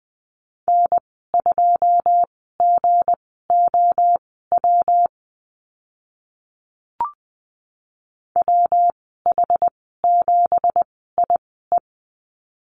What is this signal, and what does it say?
Morse code, heard as a keyed tone, about 20 wpm, spaced as for 15 wpm, 700 Hz. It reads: N2GOW WH7IE
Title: 167 Lesson 27g - Number 7 - Call Signs 20wpm